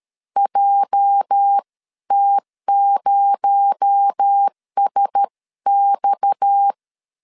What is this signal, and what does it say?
Morse code, heard as a keyed tone, about 13 wpm, 800 Hz. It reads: JT0SX
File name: CW / DET / 1 9 1500C S C 12wpm 3811